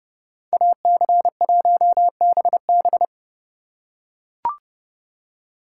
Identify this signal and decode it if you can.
Morse code, heard as a keyed tone, about 30 wpm, 700 Hz. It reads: AC1BB